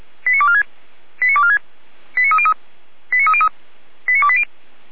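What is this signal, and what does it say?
Five-tone selective call (ZVEI), heard as quick runs of five tones, tones of about 70 ms. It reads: 89246 89246 89303 89303 89280
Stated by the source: Five-tone ZVEI selective call received by SDR, from the audio server of Feuerwehr Orlamünde fire brigade.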